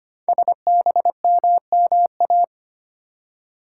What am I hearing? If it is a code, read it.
Morse code, heard as a keyed tone, about 25 wpm, 700 Hz. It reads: SBMMA
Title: Code Groups - 5 Letters - 25wpm